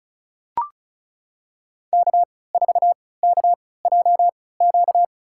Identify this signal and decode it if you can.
Morse code, heard as a keyed tone, about 36 wpm, spaced as for 20 wpm, 690 Hz. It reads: K4KJQ
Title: USA Call Signs - 3x Character Spacing - 35wpm